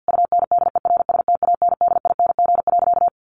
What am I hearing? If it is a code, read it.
Morse code, heard as a keyed tone, about 50 wpm, 700 Hz. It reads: UNDERSTANDING.
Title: Sentences from Top 1400 Words 50wpm